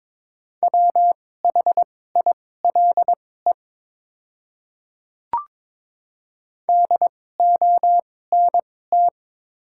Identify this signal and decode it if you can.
Morse code, heard as a keyed tone, about 22 wpm, spaced as for 16 wpm, 700 Hz. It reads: WHILE DONT
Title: 2x Character Spacing - Top 300 Words - 22wpm